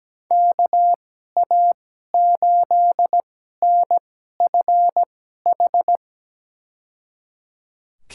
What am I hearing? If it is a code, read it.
Morse code, heard as a keyed tone, about 17 wpm, spaced as for 12 wpm, 700 Hz. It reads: KA8NFH
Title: USA Call Signs - 6 Characters - 2x Character Spacing - 17wpm